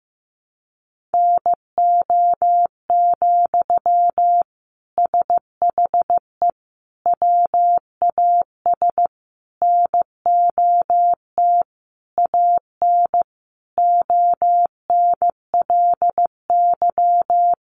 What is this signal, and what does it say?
Morse code, heard as a keyed tone, about 15 wpm, 700 Hz. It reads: NO, SHE WAS NOT AN ONLY
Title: Sentences from Top 800 Words 15wpm